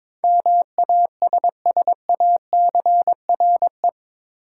Morse code, heard as a keyed tone, about 22 wpm, 700 Hz. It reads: MASSACRE
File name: practice / 22wpm / Eight Letter Words - 22wpm